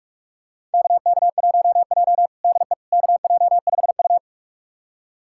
Morse code, heard as a keyed tone, about 46 wpm, 700 Hz. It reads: KK1J DE KJ5U
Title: USA Callsign DE Callsign - 4 Characters - 45wpm